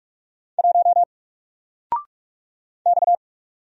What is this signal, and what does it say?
Morse code, heard as a keyed tone, about 46 wpm, 700 Hz. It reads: 1 X
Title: Single Letter-Number 45wpm